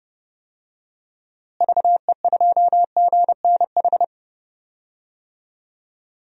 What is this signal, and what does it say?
Morse code, heard as a keyed tone, about 30 wpm, 700 Hz. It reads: VE2GNH
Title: DXCC-100 Call Signs - 6 Characters - Repeated 3x - 30wpm